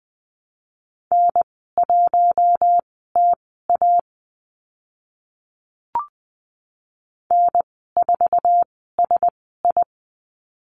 Morse code, heard as a keyed tone, about 20 wpm, spaced as for 14 wpm, 700 Hz. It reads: N1TA N4SI
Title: 055 Lesson 10g - Letter r - Call Signs 20wpm